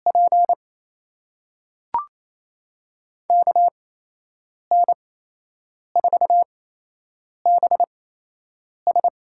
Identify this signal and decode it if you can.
Morse code, heard as a keyed tone, about 28 wpm, spaced as for 8 wpm, 700 Hz. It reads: P KN4BS
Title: USA Call Signs - 6 Characters - 8x Character Spacing - 28wpm